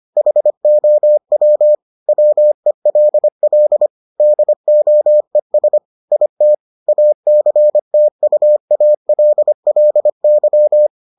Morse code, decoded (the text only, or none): HOW WELL DOES IT ACTUALLY